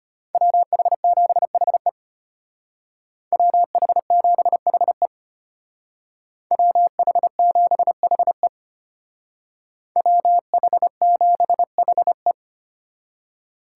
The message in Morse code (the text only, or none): WH7HE WH7HE WH7HE WH7HE